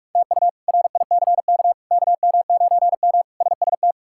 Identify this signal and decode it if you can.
Morse code, heard as a keyed tone, about 46 wpm, 700 Hz. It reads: TU RICK KM9M SST
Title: K1USN SST QSO Practice - 45wpm